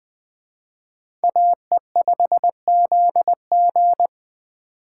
Morse code, heard as a keyed tone, about 21 wpm, 700 Hz. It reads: AE5ZG